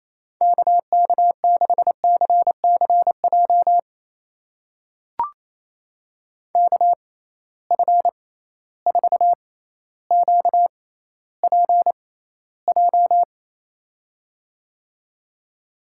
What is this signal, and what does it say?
Morse code, heard as a keyed tone, about 28 wpm, spaced as for 14 wpm, 700 Hz. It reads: KK6CCJ KF4QPJ
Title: USA Call Signs - 6 Characters - 6x Character Spacing - 28wpm